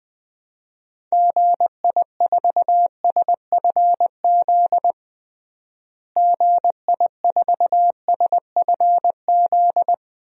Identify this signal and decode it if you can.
Morse code, heard as a keyed tone, about 20 wpm, 700 Hz. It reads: GI4SFZ GI4SFZ
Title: DXCC-100 Call Signs - 6 Characters - Repeated 2x - 20wpm